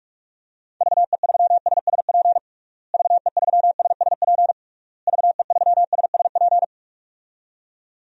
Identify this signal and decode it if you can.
Morse code, heard as a keyed tone, about 41 wpm, 700 Hz. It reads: VE3SSP VE3SSP VE3SSP
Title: DXCC-100 Call Signs - 6 Characters - Repeated 3x - 45wpm